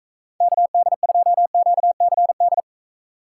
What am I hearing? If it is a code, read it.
Morse code, heard as a keyed tone, about 45 wpm, 700 Hz. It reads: KD2QCD